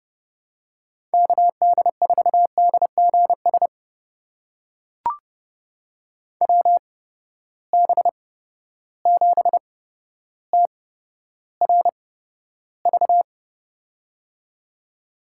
Morse code, heard as a keyed tone, about 29 wpm, spaced as for 13 wpm, 700 Hz. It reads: KD4DGS WB7TRV
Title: USA Call Signs - 6 Characters - 8x Character Spacing - 30wpm